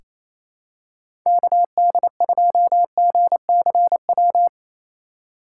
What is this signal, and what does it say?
Morse code, heard as a keyed tone, about 28 wpm, 690 Hz. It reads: KD2GCW
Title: USA Call Signs - 6 Characters - 4x Character Spacing - 28wpm